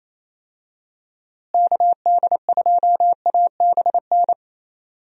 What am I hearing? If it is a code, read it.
Morse code, heard as a keyed tone, about 28 wpm, 700 Hz. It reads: KD2ABN
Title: USA Call Signs - 6 Characters - Repeated 2x - 28wpm